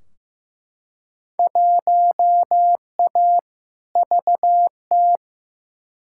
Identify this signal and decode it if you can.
Morse code, heard as a keyed tone, about 15 wpm, 700 Hz. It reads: 1A VT
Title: Field Day Exchanges Encore - 15wpm